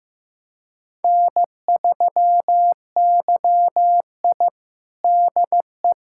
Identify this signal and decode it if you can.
Morse code, heard as a keyed tone, about 15 wpm, 700 Hz. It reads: N3YI DE
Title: USA Callsign DE Callsign - 4 Characters - 15wpm